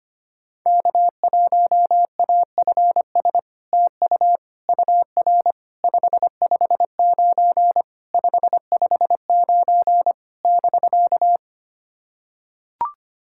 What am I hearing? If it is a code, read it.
Morse code, heard as a keyed tone, about 25 wpm, 690 Hz. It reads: K1AFS TU UR 559 559 <BK>
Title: SOTA QSO Practice 25wpm